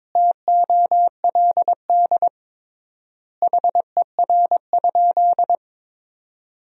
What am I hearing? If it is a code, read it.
Morse code, heard as a keyed tone, about 22 wpm, 700 Hz. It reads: TOLD HER?
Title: 3x Word-Spacing - Sentences from Top 400 Words - 22wpm